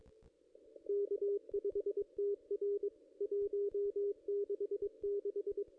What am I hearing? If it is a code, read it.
Morse code, heard as a keyed tone, about 22 wpm, 440 Hz, with radio noise. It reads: K5TR 166
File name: contests / ss-cw-k5tr-op-n5rz -1600